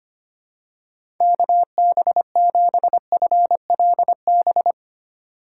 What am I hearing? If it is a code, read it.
Morse code, heard as a keyed tone, about 25 wpm, 700 Hz. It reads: KB7FLB
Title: USA Call Signs - 6 Characters - 4x Character Spacing - 25wpm